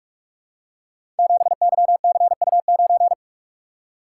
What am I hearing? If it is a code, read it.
Morse code, heard as a keyed tone, about 44 wpm, 700 Hz. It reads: 7YCU9
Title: Code Groups - 5 Letter:Numbers - 45wpm